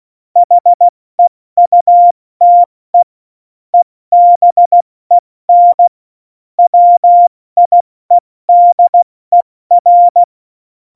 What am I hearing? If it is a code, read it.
Morse code, heard as a keyed tone, about 16 wpm, 700 Hz. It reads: HEUTE EBEN WIEDER
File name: Morsekurs